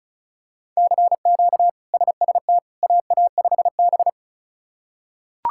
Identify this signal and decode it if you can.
Morse code, heard as a keyed tone, about 34 wpm, 700 Hz. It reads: CQ SST AA5B
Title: K1USN SST QSO Practice - 35wpm